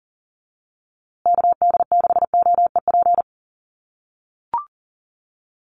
Morse code, heard as a keyed tone, about 40 wpm, 700 Hz. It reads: KD6OEP
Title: USA Call Signs Repeated 2x 40wpm